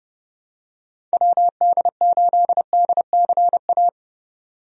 Morse code, heard as a keyed tone, about 30 wpm, 700 Hz. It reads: WD8DCA
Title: USA Call Signs - 6 Characters - 30wpm